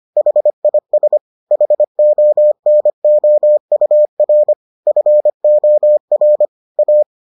Morse code, heard as a keyed tone, about 25 wpm, 600 Hz. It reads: HIS HONOUR FOR A